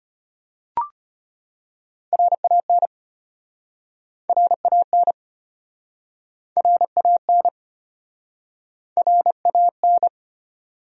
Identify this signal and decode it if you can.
Morse code, heard as a keyed tone, about 37 wpm, 700 Hz. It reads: RAN RAN RAN RAN